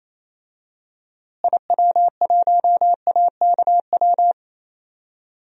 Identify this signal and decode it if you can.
Morse code, heard as a keyed tone, about 28 wpm, 700 Hz. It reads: IW1AKW
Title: DXCC-100 Call Signs - 6 Characters - Repeated 3x - 28wpm